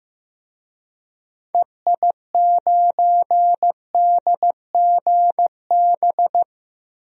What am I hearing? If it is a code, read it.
Morse code, heard as a keyed tone, about 15 wpm, 700 Hz. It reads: EI9DGB
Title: DXCC-100 Call Signs - 6 Characters - Repeated 2x - 15wpm